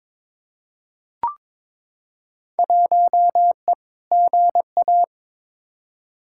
Morse code, heard as a keyed tone, about 22 wpm, 700 Hz. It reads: 1E GA
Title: Field Day Exchanges - 22wpm